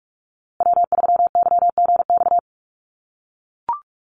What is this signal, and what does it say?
Morse code, heard as a keyed tone, about 44 wpm, 690 Hz. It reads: W3YGX
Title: USA Call Signs Repeated 3x 45wpm